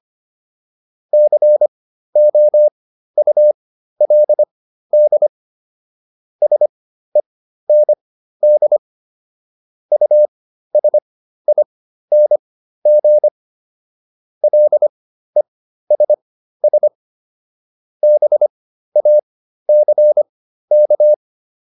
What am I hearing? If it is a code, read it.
Morse code, heard as a keyed tone, about 25 wpm, spaced as for 13 wpm, 600 Hz. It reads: COULD SEND USING LESS BACK